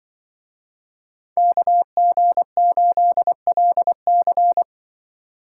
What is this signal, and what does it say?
Morse code, heard as a keyed tone, about 25 wpm, 700 Hz. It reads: KG8LC
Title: Speed Racing - USA 2x2 Call Signs 20wpm